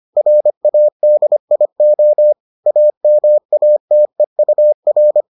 Morse code, heard as a keyed tone, about 25 wpm, 600 Hz. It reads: RADIO AMATEUR